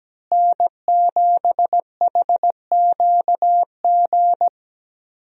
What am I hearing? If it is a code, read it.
Morse code, heard as a keyed tone, about 17 wpm, 700 Hz. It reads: N7HQG
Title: USA Call Signs - 5 Characters - Repeated 3x - 17wpm